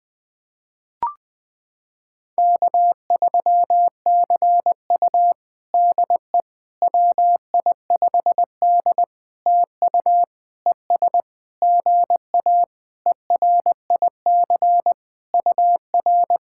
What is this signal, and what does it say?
Morse code, heard as a keyed tone, about 20 wpm, 700 Hz. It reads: K3CU DE WI5D TU ES GA ERIC UR
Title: POTA QSO Practice - 20wpm